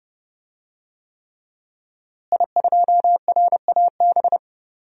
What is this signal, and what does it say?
Morse code, heard as a keyed tone, about 30 wpm, 700 Hz. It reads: I2RAB
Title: DXCC-100 Call Signs - 5 Characters - Repeated 3x - 30wpm